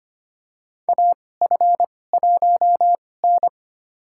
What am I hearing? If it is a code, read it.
Morse code, heard as a keyed tone, about 25 wpm, spaced as for 18 wpm, 700 Hz. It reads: AF1N
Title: USA Call Signs - 4 Characters - 2x Character Spacing - 25wpm